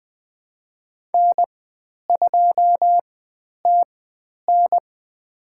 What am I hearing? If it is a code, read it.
Morse code, heard as a keyed tone, about 20 wpm, spaced as for 10 wpm, 700 Hz. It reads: N2TN
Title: 095 Lesson 16g - Letter c - Call Signs 20wpm